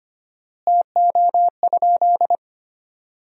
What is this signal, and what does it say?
Morse code, heard as a keyed tone, about 25 wpm, 700 Hz. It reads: TO?
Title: 4x Word-Spacing - Sentences from Top 500 Words - 25wpm